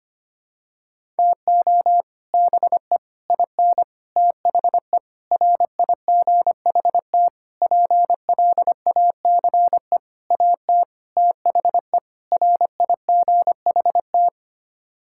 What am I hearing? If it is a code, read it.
Morse code, heard as a keyed tone, about 25 wpm, 700 Hz. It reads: TO BE IN THE RIGHT PLACE AT THE RIGHT
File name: Sets of 9 Words - 25wpm